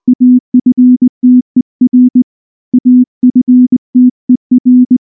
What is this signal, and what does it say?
Morse code, heard as a keyed tone, about 19 wpm, 250 Hz. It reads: AFTER AFTER
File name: morse / morse-training-20WPM-9